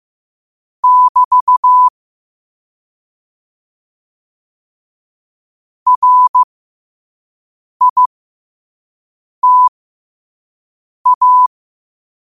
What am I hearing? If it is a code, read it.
Morse code, heard as a keyed tone, about 15 wpm, spaced as for 5 wpm, 1000 Hz. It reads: <BT> RITA